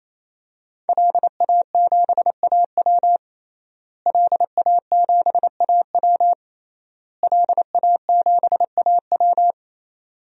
Morse code, heard as a keyed tone, about 28 wpm, 700 Hz. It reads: LA7AW LA7AW LA7AW
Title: DXCC-100 Call Signs - 5 Characters - Repeated 3x - 28wpm